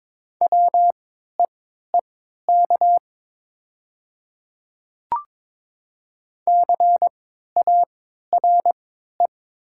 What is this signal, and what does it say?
Morse code, heard as a keyed tone, about 23 wpm, spaced as for 13 wpm, 700 Hz. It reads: WEEK CARE
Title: 3x Character Spacing - Top 500 Words - 22wpm